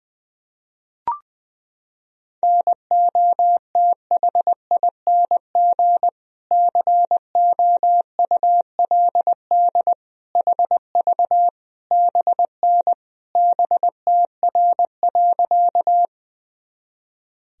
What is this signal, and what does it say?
Morse code, heard as a keyed tone, about 20 wpm, 700 Hz. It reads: NOTHING COULD HV BN BTR.
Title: Sentences from Top 500 Words and Abbreviations 20wpm